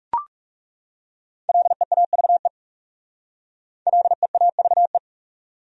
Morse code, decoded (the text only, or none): LEAVE LEAVE